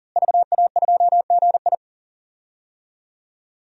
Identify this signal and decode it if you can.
Morse code, heard as a keyed tone, about 37 wpm, 700 Hz. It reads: VA2GI
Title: DXCC-100 Call Signs - 5 Characters - Repeated 2x - 40wpm